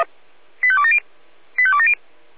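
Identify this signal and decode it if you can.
Five-tone selective call (ZVEI), heard as quick runs of five tones, tones of about 70 ms. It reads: 85270 85270
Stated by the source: ZVEI five-tone selective call received with an SDR.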